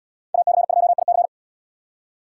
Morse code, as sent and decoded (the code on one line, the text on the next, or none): .- -.-. ..--- .. -.-.
AC2IC